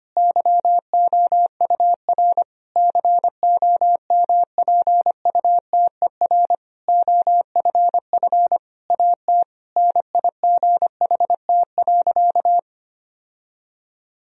Morse code, heard as a keyed tone, about 25 wpm, 700 Hz. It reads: YOUR COMPUTER OFF AT NIGHT.
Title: Sentences from Top 1000 Words - Review Part 3 - 25wpm